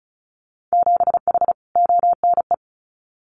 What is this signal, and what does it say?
Morse code, heard as a keyed tone, about 34 wpm, 700 Hz. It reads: 7H ONE
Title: Winter Field Day Exchanges Encore 35wpm